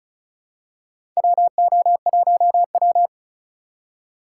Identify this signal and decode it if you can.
Morse code, heard as a keyed tone, about 36 wpm, 700 Hz. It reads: WO1W
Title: USA Call Signs - 4 Characters - 3x Character Spacing - 35wpm